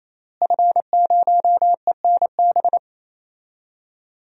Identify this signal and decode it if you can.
Morse code, heard as a keyed tone, about 28 wpm, 700 Hz. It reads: F0ENB
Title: DXCC-100 Call Signs - 5 Characters - Repeated 3x - 28wpm